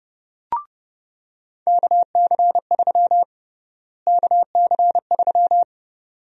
Morse code, heard as a keyed tone, about 32 wpm, 700 Hz. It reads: KC3 KC3
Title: USA Call Sign Prefixes - Repeated 2x - 30wpm